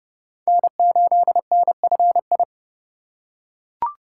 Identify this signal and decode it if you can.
Morse code, heard as a keyed tone, about 29 wpm, 700 Hz. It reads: N8NFI
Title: USA Call Signs - 5 Characters - 2x Character Spacing - 30wpm